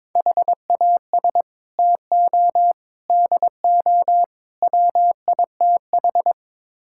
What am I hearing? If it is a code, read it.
Morse code, heard as a keyed tone, about 22 wpm, 700 Hz. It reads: HAS TO DO WITH
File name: practice / 22wpm / Build A Phrase - 5-Word Phrases - 22wpm